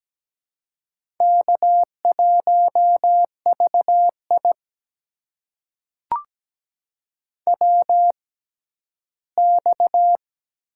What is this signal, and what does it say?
Morse code, heard as a keyed tone, about 17 wpm, spaced as for 12 wpm, 700 Hz. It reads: K1VI WX
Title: USA Call Signs - 4 Characters - 6x Character Spacing - 17wpm